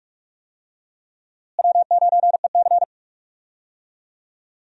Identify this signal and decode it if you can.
Morse code, heard as a keyed tone, about 44 wpm, 700 Hz. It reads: W9EC
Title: USA Call Signs - 4 Characters - Repeated 2x - 45wpm